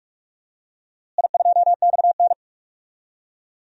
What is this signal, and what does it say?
Morse code, heard as a keyed tone, about 41 wpm, 700 Hz. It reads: I2XN
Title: DXCC-100 Call Signs - 4 Characters - Repeated 2x - 45wpm